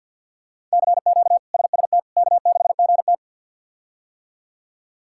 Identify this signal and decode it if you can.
Morse code, heard as a keyed tone, about 47 wpm, 700 Hz. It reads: CQ SST K6GT
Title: K1USN SST QSO Practice - 50wpm